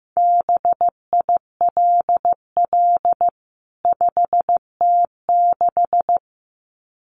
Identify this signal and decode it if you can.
Morse code, heard as a keyed tone, about 14 wpm, 700 Hz. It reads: BILL 5T6
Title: CWT Exchange 15wpm